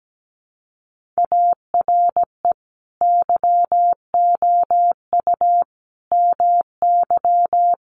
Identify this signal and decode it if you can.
Morse code, heard as a keyed tone, about 17 wpm, 700 Hz. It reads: ARE YOU MY
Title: Sentences from Top 1400 Words 17wpm